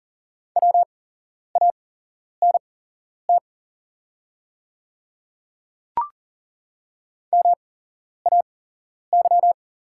Morse code, heard as a keyed tone, about 41 wpm, spaced as for 11 wpm, 700 Hz. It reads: WANT MAY
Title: 8x Character Spacing - Top 300 Words - 40wpm